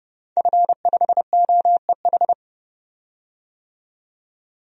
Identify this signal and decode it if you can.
Morse code, heard as a keyed tone, about 32 wpm, 700 Hz. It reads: F5OEH
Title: DXCC-100 Call Signs - 5 Characters - 30wpm